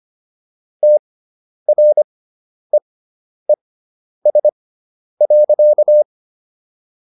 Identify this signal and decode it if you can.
Morse code, heard as a keyed tone, about 25 wpm, spaced as for 10 wpm, 600 Hz. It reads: TREES.